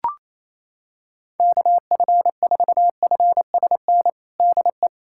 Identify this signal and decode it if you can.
Morse code, heard as a keyed tone, about 27 wpm, 700 Hz. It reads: KF4FSN DE
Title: USA Callsign DE Callsign - 6 Characters - 28wpm